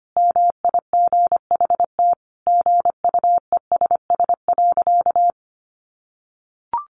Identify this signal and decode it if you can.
Morse code, heard as a keyed tone, about 25 wpm, 700 Hz. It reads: MIGHT GUESS.
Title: Sentences from Top 1500 Words 25wpm